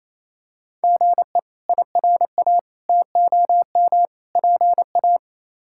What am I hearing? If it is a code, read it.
Morse code, heard as a keyed tone, about 28 wpm, 700 Hz. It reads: GE IRA TOM PA
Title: K1USN SST QSO Practice - 28wpm